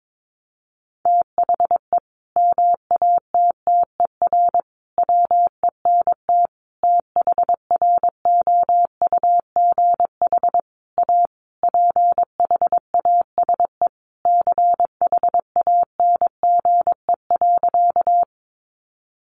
Morse code, heard as a keyed tone, about 22 wpm, 700 Hz. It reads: THE MATTER WENT THROUGH A PHASE CHANGE.
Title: Sentences from Top 1200 Words 22wpm